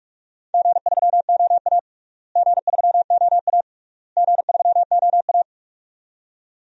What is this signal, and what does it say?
Morse code, heard as a keyed tone, about 46 wpm, 700 Hz. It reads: G3OA G3OA G3OA
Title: DXCC-100 Call Signs - 4 Characters - Repeated 3x - 45wpm